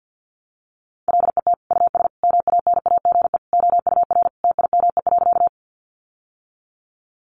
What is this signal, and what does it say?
Morse code, heard as a keyed tone, about 50 wpm, 700 Hz. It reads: LET US MANAGE OUR TIME.
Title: Sentences from Top 1400 Words 50wpm